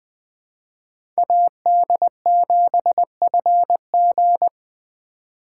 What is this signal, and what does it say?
Morse code, heard as a keyed tone, about 20 wpm, 700 Hz. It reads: AD7FG
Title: USA Call Signs - 5 Characters - 2x Character Spacing - 20wpm